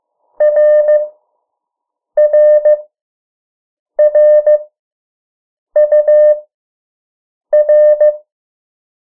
Morse code, heard as a keyed tone, about 15 wpm, spaced as for 6 wpm, 600 Hz. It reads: RRRUR
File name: morse mp3 / 03 urok 2